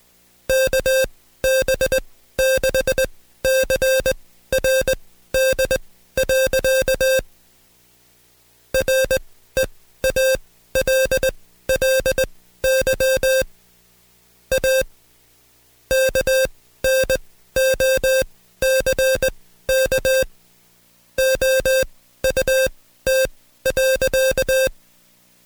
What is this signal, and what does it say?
Morse code, heard as a keyed tone, about 20 wpm, spaced as for 14 wpm, 530 Hz. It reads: KB6CRD. REALLY A KNOCK OUT.
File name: Code Quick / AudioDownld2 / Seg133-Stepper6 13WPM